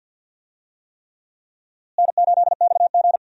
Morse code, heard as a keyed tone, about 51 wpm, 700 Hz. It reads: N8XG
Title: USA Call Signs - 4 Characters - Repeated 3x - 50wpm